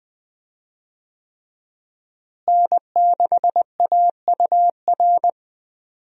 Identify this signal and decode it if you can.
Morse code, heard as a keyed tone, about 21 wpm, 700 Hz. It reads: N6AUR
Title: USA Call Signs - 5 Characters - Repeated 3x - 20wpm